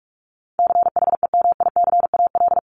Morse code, heard as a keyed tone, about 44 wpm, 700 Hz. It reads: CHEMICAL
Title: Top 1001-1250 Words 45wpm